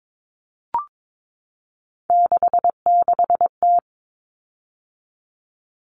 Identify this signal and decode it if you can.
Morse code, heard as a keyed tone, about 22 wpm, 700 Hz. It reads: 66T
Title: Common Cut Numbers - 3-Digit 22wpm